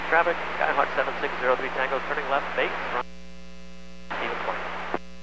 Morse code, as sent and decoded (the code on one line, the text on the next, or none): none